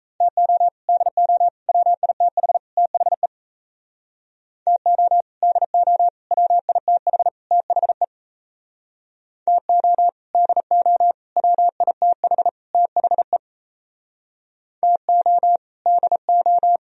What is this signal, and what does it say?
Morse code, heard as a keyed tone, about 34 wpm, 690 Hz. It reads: TO DO WITH THE TO DO WITH THE TO DO WITH THE TO DO
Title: Speed-Racing - Sets of 4 Words 28wpm